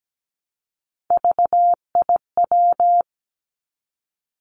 Morse code, heard as a keyed tone, about 17 wpm, 700 Hz. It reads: VIW
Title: Three Challenging Letter-Number Combinations 17wpm